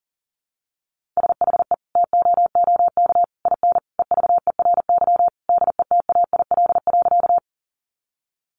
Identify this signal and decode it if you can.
Morse code, heard as a keyed tone, about 37 wpm, 700 Hz. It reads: SHE TOOK IN EVERY DETAIL.
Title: Sentences from Top 1500 Words 40wpm